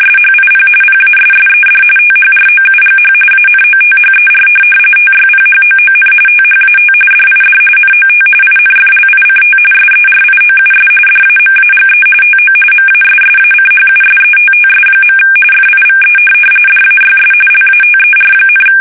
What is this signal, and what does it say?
RTTY, about 45.45 baud. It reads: SHIP PASSED THROUGH ANOTHER TYPHOON THAT DID NOT INFLICT SERIOUS DAMAGE TO MASSACHUSETTS. FIVE DAYS LATER,
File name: fsk850 / USS-Massachusetts